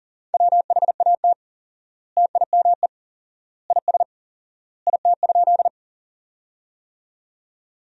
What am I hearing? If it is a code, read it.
Morse code, heard as a keyed tone, about 40 wpm, 700 Hz. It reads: WHAT TIME IS IT?
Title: 4x Word-Spacing - Sentences from Top 100 Words - 40wpm